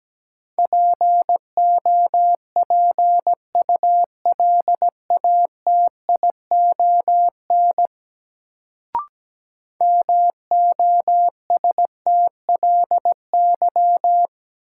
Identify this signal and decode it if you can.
Morse code, heard as a keyed tone, about 17 wpm, 700 Hz. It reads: POPULATION MOSTLY
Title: Sentences from Top 1300 Words - New Word Drill Alt - 17wpm